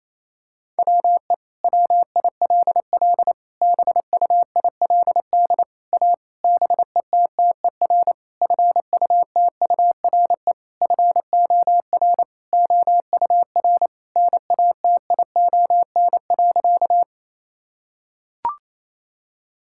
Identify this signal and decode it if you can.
Morse code, heard as a keyed tone, about 28 wpm, 700 Hz. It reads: WE WILL BUILD A BETTER FUTURE FOR OUR NATION.
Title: Sentences from Top 1000 Words - Review Part 6 - 28wpm